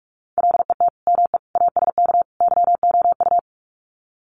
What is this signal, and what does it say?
Morse code, heard as a keyed tone, about 44 wpm, 700 Hz. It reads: LET ME ASK YOU
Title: Sets of 4 Words 45wpm